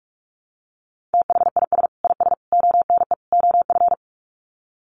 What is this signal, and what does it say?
Morse code, heard as a keyed tone, about 44 wpm, 700 Hz. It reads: THIS IS ONE OF
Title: Sets of 4 Words 45wpm